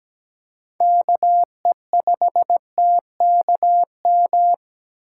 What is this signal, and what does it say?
Morse code, heard as a keyed tone, about 17 wpm, 700 Hz. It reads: KE5TKM
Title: USA Call Signs - 6 Characters - 2x Character Spacing - 17wpm